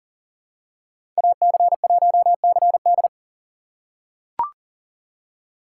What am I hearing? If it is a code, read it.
Morse code, heard as a keyed tone, about 40 wpm, 700 Hz. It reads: AC1CD
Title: USA Call Signs - 2x Character Spacing - 40wpm